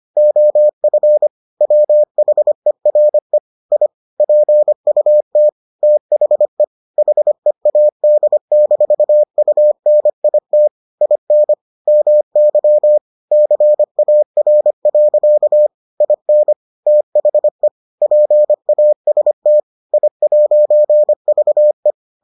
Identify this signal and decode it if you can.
Morse code, heard as a keyed tone, about 25 wpm, 600 Hz. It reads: OF WHERE I PUT THE HEAD-UNIT IN MY CAR. IN THE PAST I'VE